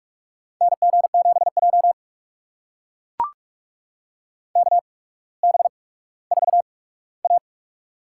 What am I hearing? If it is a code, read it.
Morse code, heard as a keyed tone, about 44 wpm, spaced as for 19 wpm, 700 Hz. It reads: NG7J KB4A
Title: USA Call Signs - 8x Character Spacing - 45wpm